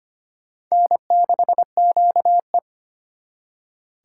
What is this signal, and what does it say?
Morse code, heard as a keyed tone, about 33 wpm, 700 Hz. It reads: N6QE